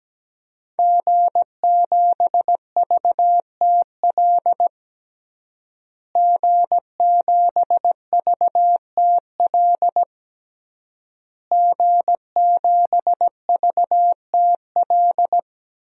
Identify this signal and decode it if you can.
Morse code, heard as a keyed tone, about 17 wpm, 700 Hz. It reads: G7VTL G7VTL G7VTL
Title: DXCC-100 Call Signs - 5 Characters - Repeated 3x - 17wpm